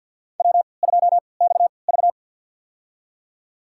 Morse code, heard as a keyed tone, about 50 wpm, spaced as for 28 wpm, 700 Hz. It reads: W2XV
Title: USA Call Signs - 4 Characters - 3x Character Spacing - 50wpm